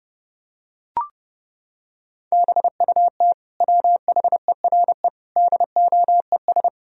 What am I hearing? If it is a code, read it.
Morse code, heard as a keyed tone, about 31 wpm, 700 Hz. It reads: BUT WHERE DOES
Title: Sentences from Top 1000 Words - Review Part 2 - 30wpm